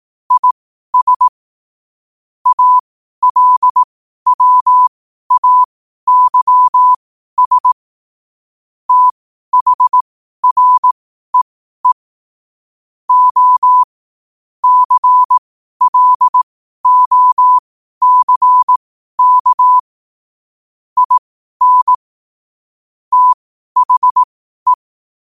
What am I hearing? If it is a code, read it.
Morse code, heard as a keyed tone, about 18 wpm, spaced as for 12 wpm, 1000 Hz. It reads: IS ALWAYS THREE O CLOCK IN THE
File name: qotdfiles / 12wpm